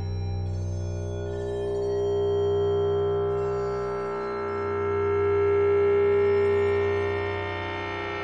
Dial tone (Japan), 400 Hz, with noise.
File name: wrct-log hi- 10